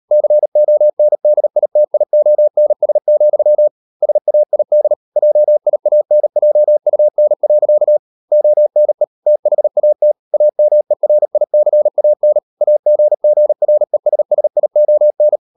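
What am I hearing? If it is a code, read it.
Morse code, heard as a keyed tone, about 38 wpm, 600 Hz. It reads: CONDITIONS, SAID JIANJUN. ONE THAT AMERICAN AGGRESSION
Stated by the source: Morse code practice at 35–39 wpm sending English text of Ken Follett's novel Never, from CW med Gnister.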